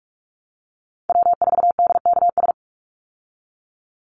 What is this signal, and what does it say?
Morse code, heard as a keyed tone, about 44 wpm, 700 Hz. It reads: W4DKS
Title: Call Signs Repeated 3x - 45wpm